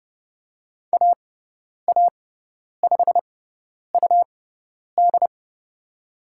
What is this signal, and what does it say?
Morse code, heard as a keyed tone, about 30 wpm, spaced as for 10 wpm, 700 Hz. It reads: AA5UD
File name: Lesson 17g - Letter u - Call Signs 30wpm